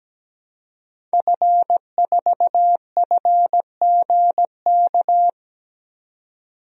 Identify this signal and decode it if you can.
Morse code, heard as a keyed tone, about 17 wpm, 700 Hz. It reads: F4FGK
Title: DXCC-100 Call Signs - 5 Characters - 17wpm